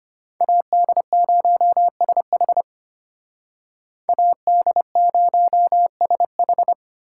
Morse code, heard as a keyed tone, about 32 wpm, 700 Hz. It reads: AD0SH AD0SH